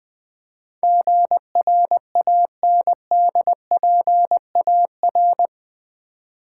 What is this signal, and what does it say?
Morse code, heard as a keyed tone, about 20 wpm, 700 Hz. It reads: GRANDPAR
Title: Build A Word - 4 Letters At A Time - 11-Letter Words - 20wpm